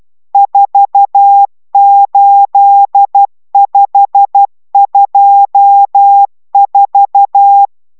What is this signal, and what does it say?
Morse code, heard as a keyed tone, about 12 wpm, 800 Hz. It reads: 48524